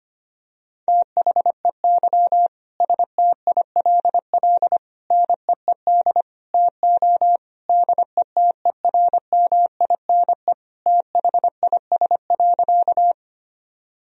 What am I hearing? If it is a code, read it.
Morse code, heard as a keyed tone, about 25 wpm, 700 Hz. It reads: THEY STILL NEED TO DETERMINE THIS.
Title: Sentences from Top 1000 Words - Review Part 4 - 25wpm